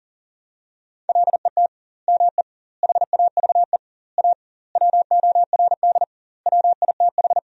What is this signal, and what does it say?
Morse code, heard as a keyed tone, about 40 wpm, 700 Hz. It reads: LET ME HAVE A WORD WITH
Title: 2x Word-Spacing - Sentences from Top 300 Words - 40wpm